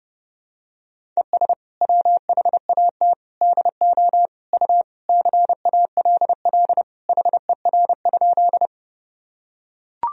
Morse code, heard as a keyed tone, about 32 wpm, 700 Hz. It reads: ES WHAT DO U CALL HER?
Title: Sentences from Top 100 Words and Abbreviations 30wpm